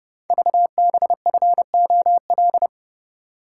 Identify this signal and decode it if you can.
Morse code, heard as a keyed tone, about 29 wpm, 700 Hz. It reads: VBFOL
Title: Code Groups - 5 Letters - 30wpm